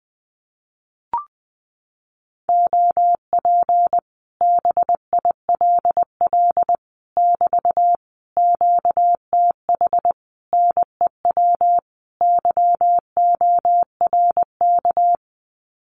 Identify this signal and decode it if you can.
Morse code, heard as a keyed tone, about 19 wpm, 700 Hz. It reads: OP BILL <BT> QTH NEW YORK
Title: Common QSO Phrases Encore 20wpm